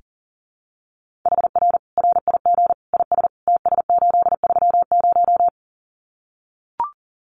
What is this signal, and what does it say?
Morse code, heard as a keyed tone, about 40 wpm, 700 Hz. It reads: HR RIG IS TS830
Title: Common QSO Phrases 40wpm